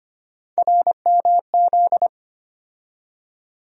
Morse code, heard as a keyed tone, about 25 wpm, 700 Hz. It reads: RMZ
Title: USA Call Sign Postfixes - 3 Letters 25wpm